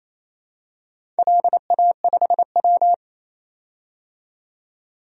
Morse code, heard as a keyed tone, about 27 wpm, 700 Hz. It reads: LA5W
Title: DXCC-100 Call Signs - 4 Characters - 28wpm